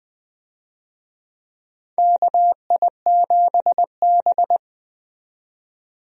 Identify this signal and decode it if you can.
Morse code, heard as a keyed tone, about 20 wpm, 700 Hz. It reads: KI7B
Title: USA Call Signs - 4 Characters - 20wpm